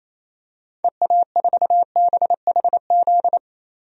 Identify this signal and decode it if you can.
Morse code, heard as a keyed tone, about 28 wpm, 700 Hz. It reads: EA4BHZ
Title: DXCC-100 Call Signs - 6 Characters - 28wpm